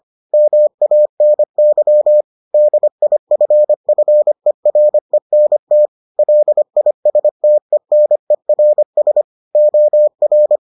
Morse code, heard as a keyed tone, about 25 wpm, 600 Hz. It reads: MANY DIFFERENT LISTENERS OR